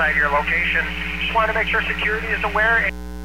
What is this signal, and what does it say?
Emergency Alert System (SAME) tones, heard as harsh buzzing bursts, with speech. It is heard from 0.5 to 3.0 s.